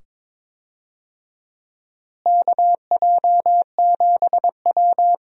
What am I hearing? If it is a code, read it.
Morse code, heard as a keyed tone, about 22 wpm, 700 Hz. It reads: KJ7W
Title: USA Call Signs - 4 Characters - Repeated 2x - 22wpm